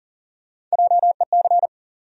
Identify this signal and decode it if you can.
Morse code, heard as a keyed tone, about 40 wpm, 700 Hz. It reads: JEC
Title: USA Call Sign Postfixes - 3 Letters 40wpm